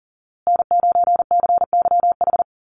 Morse code, heard as a keyed tone, about 40 wpm, 700 Hz. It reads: N9CYH
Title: USA Call Signs Repeated 3x 40wpm